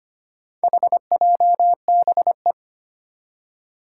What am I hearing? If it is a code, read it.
Morse code, heard as a keyed tone, about 25 wpm, 700 Hz. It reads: HJBE
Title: Four Challenging Letter-Number Combinations 25wpm